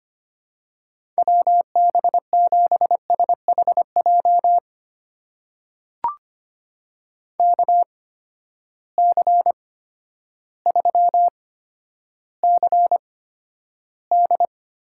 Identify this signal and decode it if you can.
Morse code, heard as a keyed tone, about 25 wpm, spaced as for 11 wpm, 700 Hz. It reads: WB7SHJ KC3CD
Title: USA Call Signs - 6 Characters - 8x Character Spacing - 25wpm